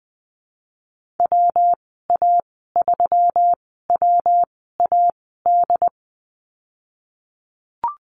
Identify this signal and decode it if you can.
Morse code, heard as a keyed tone, about 20 wpm, spaced as for 15 wpm, 700 Hz. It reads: WA3WAD
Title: 186 Lesson 30g - Letter b - Call Signs 20wpm